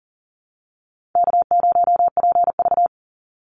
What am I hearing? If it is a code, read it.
Morse code, heard as a keyed tone, about 40 wpm, 700 Hz. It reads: K0PV
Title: Easy Call Signs Repeated 2x - 40wpm